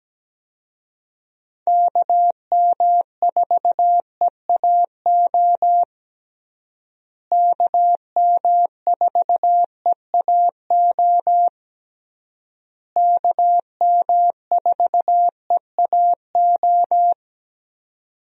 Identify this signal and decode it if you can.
Morse code, heard as a keyed tone, about 17 wpm, 700 Hz. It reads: KM4EAO KM4EAO KM4EAO
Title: USA Call Signs - 6 Characters - Repeated 3x - 17wpm